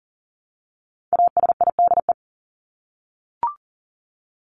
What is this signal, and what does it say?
Morse code, heard as a keyed tone, about 40 wpm, 700 Hz. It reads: ASIDE
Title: Top 1751-2000 Words 40wpm